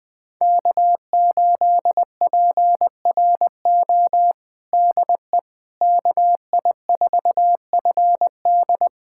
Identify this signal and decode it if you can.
Morse code, heard as a keyed tone, about 20 wpm, 700 Hz. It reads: K8PRO DE KI4FD
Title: USA Callsign DE Callsign - 5 Characters - 20wpm